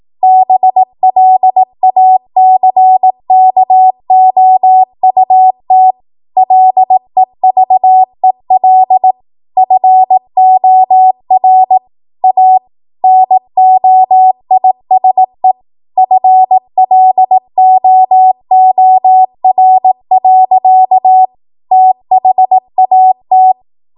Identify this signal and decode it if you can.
Morse code, heard as a keyed tone, about 18 wpm, 750 Hz. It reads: BLACKOUT LEVEL FOR A NOISE FLOOR. THAT
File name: Archive / 18 WPM / 18WPM